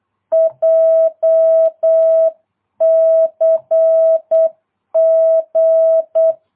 Morse code, heard as a keyed tone, about 8 wpm, 650 Hz. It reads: JCG